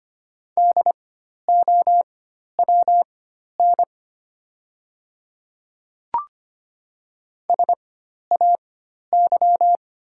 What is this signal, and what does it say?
Morse code, heard as a keyed tone, about 25 wpm, spaced as for 12 wpm, 700 Hz. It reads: DOWN SAY
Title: 4x Character Spacing - Top 300 Words - 25wpm